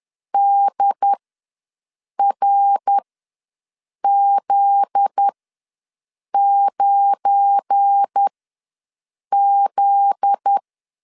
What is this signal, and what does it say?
Morse code, heard as a keyed tone, about 10 wpm, spaced as for 6 wpm, 800 Hz. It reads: DRZ9Z
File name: LEZIONI / 7 GF9Z / 7 GF9Z ws 10wpm 3919